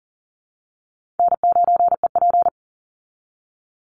Morse code, heard as a keyed tone, about 40 wpm, 700 Hz. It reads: N9EP
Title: Easy Call Signs Repeated 3x - 40wpm